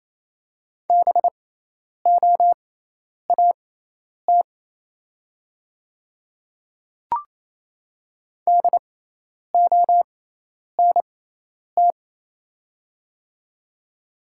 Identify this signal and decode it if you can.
Morse code, heard as a keyed tone, about 28 wpm, spaced as for 10 wpm, 700 Hz. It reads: BOAT DONT
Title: 6x Character Spacing - Top 500 Words - 28wpm